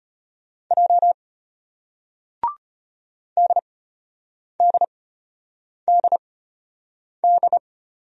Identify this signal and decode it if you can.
Morse code, heard as a keyed tone, about 33 wpm, spaced as for 8 wpm, 700 Hz. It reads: J DDDD